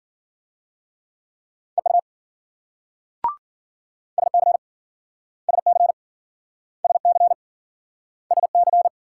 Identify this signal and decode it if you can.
Morse code, heard as a keyed tone, about 53 wpm, 700 Hz. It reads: EU SC SC SC SC